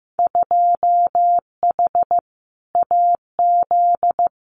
Morse code, heard as a keyed tone, about 15 wpm, 700 Hz. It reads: 2H AZ
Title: Winter Field Day Exchanges 15wpm